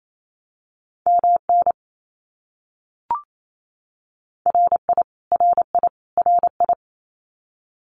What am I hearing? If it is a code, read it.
Morse code, heard as a keyed tone, about 28 wpm, 700 Hz. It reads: MN RI RI RI
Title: US States Repeated 3x 28wpm